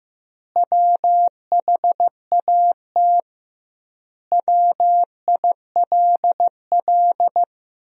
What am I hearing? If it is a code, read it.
Morse code, heard as a keyed tone, about 15 wpm, 700 Hz. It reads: WHAT WILL
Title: 2x Word-Spacing - Sentences from Top 500 Words - 15wpm